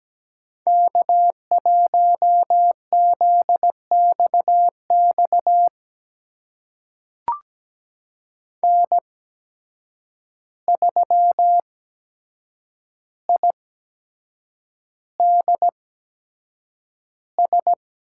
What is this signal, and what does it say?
Morse code, heard as a keyed tone, about 17 wpm, spaced as for 7 wpm, 700 Hz. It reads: K1ZXX N3IDS
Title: USA Call Signs - 5 Characters - 8x Character Spacing - 17wpm